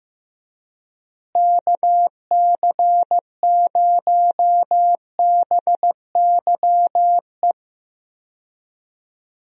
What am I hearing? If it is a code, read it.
Morse code, heard as a keyed tone, about 16 wpm, 700 Hz. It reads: KC0BYE
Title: USA Call Signs - 6 Characters - Repeated 2x - 15wpm